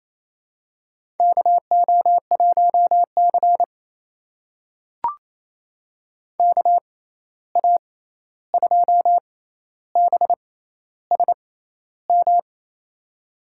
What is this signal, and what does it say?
Morse code, heard as a keyed tone, about 28 wpm, spaced as for 13 wpm, 700 Hz. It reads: KO1C KA2BSM
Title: USA Call Signs - 6x Character Spacing - 28wpm